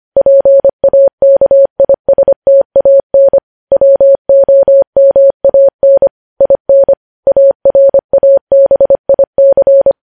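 Morse code, heard as a keyed tone, about 25 wpm, 550 Hz. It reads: PAKISTAN WOMAN IN ARABIC
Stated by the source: Morse code news headlines at 25 wpm.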